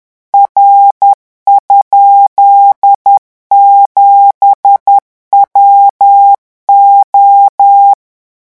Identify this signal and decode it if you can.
Morse code, heard as a keyed tone, about 10 wpm, 800 Hz. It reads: R?7WO
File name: morse mp3 / revision13-vitesse 10 mots